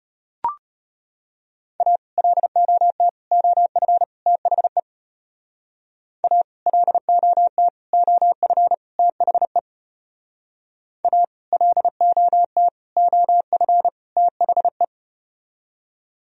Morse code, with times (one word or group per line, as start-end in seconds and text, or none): 1.8-2.0 A
2.2-3.1 LOT
3.3-4.0 OF
4.3-4.8 THE
6.2-6.4 A
6.7-7.7 LOT
7.9-8.7 OF
9.0-9.6 THE
11.0-11.2 A
11.5-12.7 LOT
13.0-13.9 OF
14.2-14.8 THE